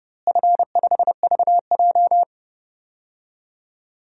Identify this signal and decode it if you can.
Morse code, heard as a keyed tone, about 31 wpm, 700 Hz. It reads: F5VJ
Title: DXCC-100 Call Signs - 4 Characters - Repeated 2x - 30wpm